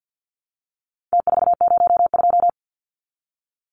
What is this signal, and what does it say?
Morse code, heard as a keyed tone, about 50 wpm, 700 Hz. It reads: T402
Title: Numbers - Contest Style - 4-Digit 50wpm